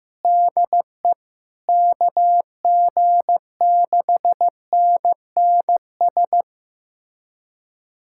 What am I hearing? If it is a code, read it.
Morse code, heard as a keyed tone, about 15 wpm, 700 Hz. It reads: DE KG6NNS
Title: USA Callsign DE Callsign 15wpm